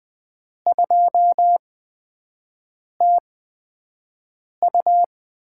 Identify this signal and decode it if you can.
Morse code, heard as a keyed tone, about 21 wpm, spaced as for 5 wpm, 700 Hz. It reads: 2TU
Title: USA Call Signs - 8x Character Spacing - 20wpm